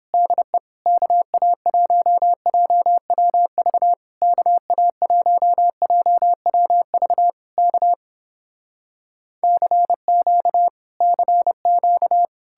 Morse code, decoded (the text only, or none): DE KA1JWV KA1JWV K CQ CQ